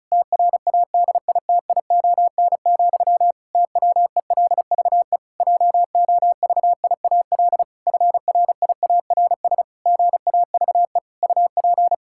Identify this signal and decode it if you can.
Morse code, heard as a keyed tone, about 36 wpm, 700 Hz. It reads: TRADITION, TWELVE JOVIAL FRIARS GAVE UP
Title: Pangram Challenge v2 - 35wpm